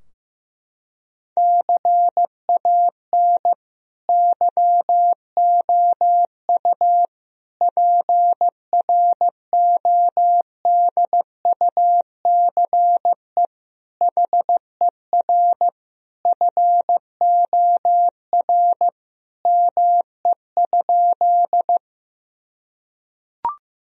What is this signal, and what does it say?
Morse code, heard as a keyed tone, about 16 wpm, 700 Hz. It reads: CAN YOU PRODUCE HER FOR ME?
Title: Sentences from Top 1000 Words - Review Part 3 - 15wpm